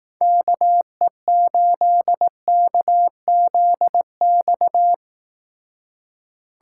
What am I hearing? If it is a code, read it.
Morse code, heard as a keyed tone, about 18 wpm, 700 Hz. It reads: KE8KZX
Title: Speed Racing - USA 2x3 Call Signs 15wpm